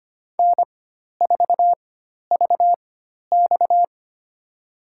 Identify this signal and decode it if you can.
Morse code, heard as a keyed tone, about 25 wpm, spaced as for 12 wpm, 700 Hz. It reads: N4VX
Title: USA Call Signs - 4 Characters - 4x Character Spacing - 25wpm